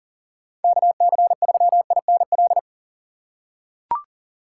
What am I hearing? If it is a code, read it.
Morse code, heard as a keyed tone, about 40 wpm, 700 Hz. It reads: KC3INL
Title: USA Call Signs - 2x Character Spacing - 40wpm